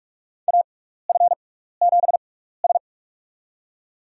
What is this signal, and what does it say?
Morse code, heard as a keyed tone, about 46 wpm, spaced as for 16 wpm, 700 Hz. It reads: AF7S
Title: USA Call Signs - 4 Characters - 6x Character Spacing - 45wpm